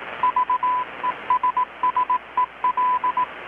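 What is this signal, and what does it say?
Morse code, heard as a keyed tone, about 18 wpm, 1000 Hz, with radio noise. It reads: VESSEL
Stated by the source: Morse from KPH's Night of Nights XX maritime radiotelegraphy event, received in Chiapas, Mexico about 3,800 km away on a Kenwood R-600 with a grounded 10 m long-wire antenna.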